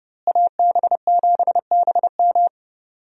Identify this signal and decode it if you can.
Morse code, heard as a keyed tone, about 30 wpm, 700 Hz. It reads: AB7BM
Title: USA Call Signs - 5 Characters - Repeated 3x - 30wpm